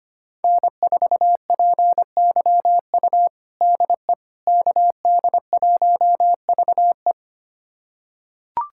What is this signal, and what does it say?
Morse code, heard as a keyed tone, about 25 wpm, 700 Hz. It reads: N4PYU DE KD1VE
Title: USA Callsign DE Callsign - 5 Characters - 25wpm